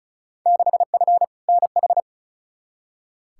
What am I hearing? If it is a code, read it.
Morse code, heard as a keyed tone, about 36 wpm, 700 Hz. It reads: 6F NH
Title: Field Day Exchanges Encore - 35wpm